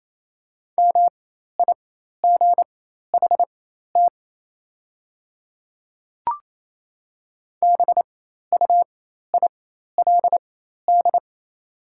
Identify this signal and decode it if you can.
Morse code, heard as a keyed tone, about 28 wpm, spaced as for 13 wpm, 700 Hz. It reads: MIGHT BUILD
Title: 4x Character Spacing - Top 300 Words - 28wpm